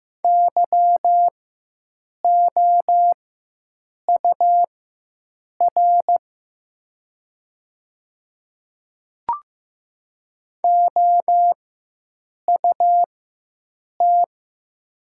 Morse code, heard as a keyed tone, about 15 wpm, spaced as for 7 wpm, 700 Hz. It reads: YOUR OUT
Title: 4x Character Spacing - Top 100 Words - 15wpm